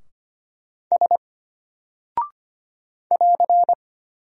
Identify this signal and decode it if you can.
Morse code, heard as a keyed tone, about 25 wpm, 700 Hz. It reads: S <AR>
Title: Single Character (Letter:Number:Punctuation:Prosign) Alt 25wpm